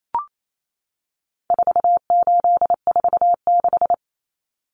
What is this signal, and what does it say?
Morse code, heard as a keyed tone, about 28 wpm, 700 Hz. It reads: 4846
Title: Common Cut Numbers - 4-Digit 28wpm